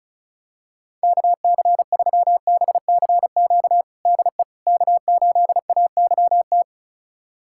Morse code, heard as a keyed tone, about 36 wpm, 700 Hz. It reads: KC3BCQ DE K8AYT
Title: USA Callsign DE Callsign 35wpm